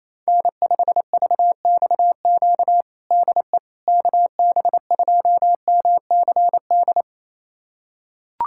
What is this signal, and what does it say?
Morse code, heard as a keyed tone, about 27 wpm, 700 Hz. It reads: N5VXQ DE KB2MCD
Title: USA Callsign DE Callsign 28wpm